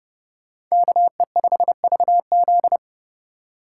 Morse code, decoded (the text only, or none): KE5VZ